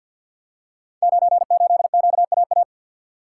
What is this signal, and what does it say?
Morse code, heard as a keyed tone, about 47 wpm, 700 Hz. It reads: 98QAA
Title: Code Groups - 5 Letter:Numbers - 50wpm